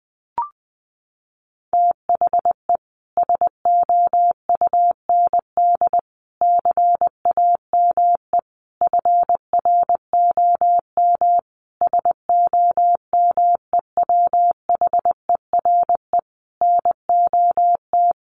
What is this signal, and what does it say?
Morse code, heard as a keyed tone, about 20 wpm, 700 Hz. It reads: THE SOUND CAME FROM SOMEWHERE NOT
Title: Sentences from Top 1400 Words 20wpm